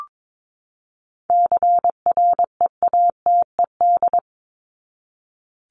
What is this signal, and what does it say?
Morse code, heard as a keyed tone, about 22 wpm, 700 Hz. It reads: CREATED
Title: ED Suffix - Top 100 Words with Base Words 22wpm